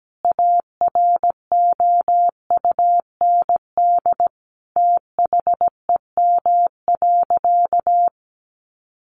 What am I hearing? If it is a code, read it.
Morse code, heard as a keyed tone, about 17 wpm, 700 Hz. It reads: AROUND THEM.
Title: Sentences from Top 600 Words 17wpm